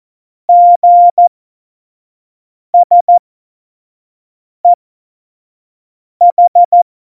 Morse code, heard as a keyed tone, about 14 wpm, spaced as for 5 wpm, 700 Hz. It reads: GSEH
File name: Morsekurs / Witze / Witz 24. November 2020, 25 F70